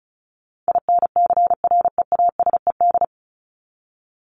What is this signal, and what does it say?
Morse code, heard as a keyed tone, about 34 wpm, 700 Hz. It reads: INCREASED
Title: ED Suffix - Top 100 Words with Base Words 35wpm